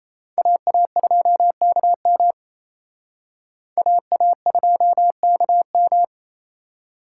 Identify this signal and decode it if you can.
Morse code, heard as a keyed tone, about 35 wpm, 700 Hz. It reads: AA2KM AA2KM
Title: Speed Racing - USA Call Signs 28wpm